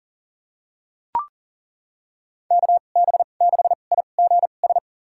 Morse code, heard as a keyed tone, about 38 wpm, spaced as for 28 wpm, 700 Hz. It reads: KB6IGS
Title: USA Call Signs - 6 Characters - 2x Character Spacing - 40wpm